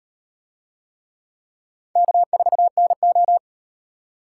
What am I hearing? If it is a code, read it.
Morse code, heard as a keyed tone, about 38 wpm, 700 Hz. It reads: K4NO